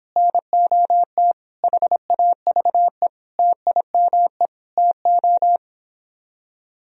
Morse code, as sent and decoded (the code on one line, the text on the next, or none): -. --- - / .... .- ...- . / - .. -- . / - ---
NOT HAVE TIME TO